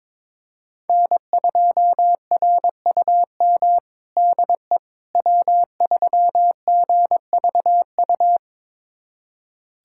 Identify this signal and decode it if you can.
Morse code, heard as a keyed tone, about 22 wpm, 700 Hz. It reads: N2RUM DE W3GVU
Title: USA Callsign DE Callsign - 5 Characters - 22wpm